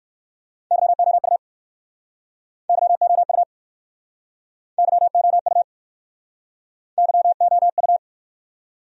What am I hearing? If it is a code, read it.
Morse code, heard as a keyed tone, about 61 wpm, 700 Hz. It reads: YOU YOU YOU YOU